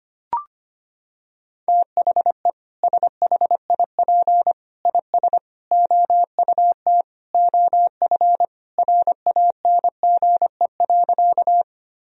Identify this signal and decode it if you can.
Morse code, heard as a keyed tone, about 25 wpm, 700 Hz. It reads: THE SHIP IS OUT OF RANGE.
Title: Sentences from Top 1000 Words - Review Part 4 - 25wpm